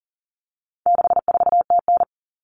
Morse code, heard as a keyed tone, about 40 wpm, 700 Hz. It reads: 64TN
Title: Common Cut Numbers - 4-Digit 40wpm